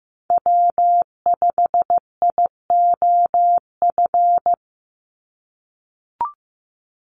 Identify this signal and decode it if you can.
Morse code, heard as a keyed tone, about 15 wpm, 700 Hz. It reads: W5IOF
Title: USA Call Signs 15wpm